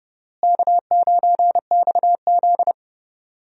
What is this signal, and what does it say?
Morse code, heard as a keyed tone, about 31 wpm, 700 Hz. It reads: K9XZ
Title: USA Call Signs - 4 Characters - Repeated 3x - 30wpm